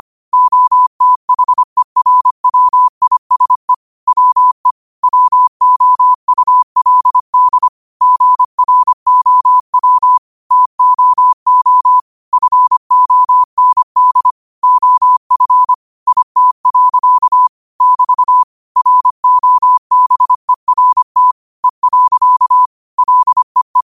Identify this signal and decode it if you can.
Morse code, heard as a keyed tone, about 25 wpm, 1000 Hz. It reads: OTHERWISE WE WOULD GROW TOO FOND OF IT. <BT> ROBERT E. LEE
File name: qotdfiles / 25wpm